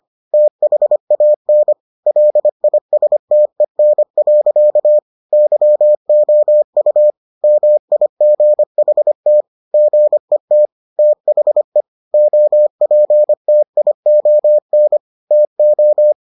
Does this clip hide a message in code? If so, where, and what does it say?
Yes: Morse code from 0.3 to 16.1 s, heard as a keyed tone, about 25 wpm, 600 Hz. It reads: THAN LISTEN. YOU MIGHT GET THE OPTION TO